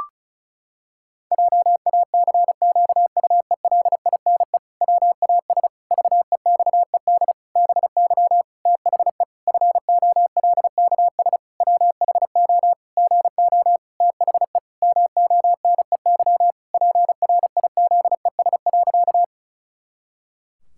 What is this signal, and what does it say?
Morse code, heard as a keyed tone, about 36 wpm, 700 Hz. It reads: JACQUELINE WAS VEXED BY THE FOLKS WHO GO THE MONEY PRIZES.
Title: Pangram Challenge v2 - 35wpm